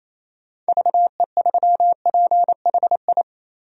Morse code, heard as a keyed tone, about 27 wpm, 700 Hz. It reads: VE3PHI
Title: DXCC-100 Call Signs - 6 Characters - 28wpm